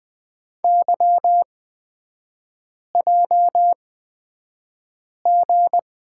Morse code, heard as a keyed tone, about 20 wpm, spaced as for 5 wpm, 700 Hz. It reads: YJG
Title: 255 Lesson 41e - Period - Three Characters 20wpm